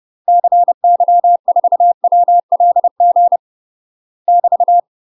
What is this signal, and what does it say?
Morse code, heard as a keyed tone, about 30 wpm, 700 Hz. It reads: CY4WLG <BT>
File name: UK MASTER QSOs TRAINING / 30WPM / UK MASTER QSOs 0039